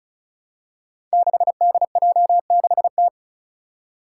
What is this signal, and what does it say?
Morse code, heard as a keyed tone, about 34 wpm, 700 Hz. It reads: 6DJ6T
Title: Five Challenging Letter-Number Combinations 35wpm